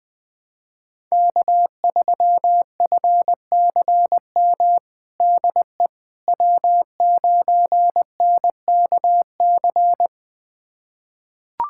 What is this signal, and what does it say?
Morse code, heard as a keyed tone, about 20 wpm, 700 Hz. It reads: K3FCM DE W9NKC
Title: USA Callsign DE Callsign - 5 Characters - 20wpm